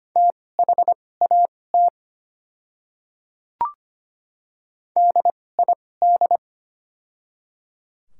Morse code, heard as a keyed tone, about 25 wpm, spaced as for 18 wpm, 700 Hz. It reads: THAT DID
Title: 2x Character Spacing - Top 200 Words - 25wpm